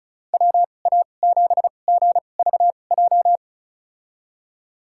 Morse code, heard as a keyed tone, about 36 wpm, spaced as for 25 wpm, 700 Hz. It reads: WA7GVJ
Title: USA Call Signs - 2x Character Spacing - 35wpm